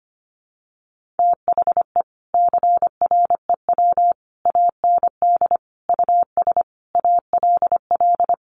Morse code, heard as a keyed tone, about 25 wpm, 700 Hz. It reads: THE CREW AND US ALL
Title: Sentences from Top 1500 Words 25wpm